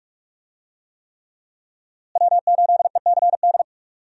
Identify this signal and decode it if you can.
Morse code, heard as a keyed tone, about 46 wpm, 700 Hz. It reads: W8ECD
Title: USA Call Signs - 5 Characters - Repeated 3x - 45wpm